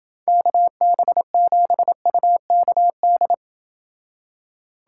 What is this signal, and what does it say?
Morse code, heard as a keyed tone, about 29 wpm, 700 Hz. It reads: KB7UKD